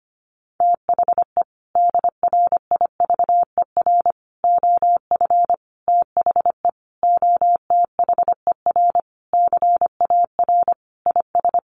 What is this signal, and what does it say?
Morse code, heard as a keyed tone, about 25 wpm, 700 Hz. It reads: THE DRIVER OF THE OTHER CAR IS
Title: Sentences from Top 1000 Words 25wpm